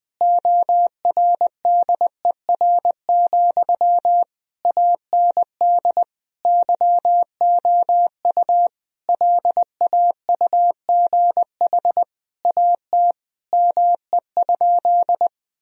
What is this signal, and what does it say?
Morse code, heard as a keyed tone, about 20 wpm, 700 Hz. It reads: ORDER, AND YOU LAUGH AT ME?
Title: Sentences from Top 1000 Words - Review Part 2 - 20wpm